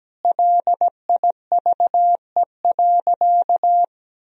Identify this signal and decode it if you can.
Morse code, heard as a keyed tone, about 17 wpm, 700 Hz. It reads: LIVE.
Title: Sentences from Top 1000 Words - Review Part 4 - 17wpm